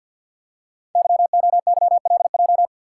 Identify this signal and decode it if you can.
Morse code, heard as a keyed tone, about 50 wpm, 700 Hz. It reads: YOYLJ
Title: Code Groups - 5 Letter:Numbers - 50wpm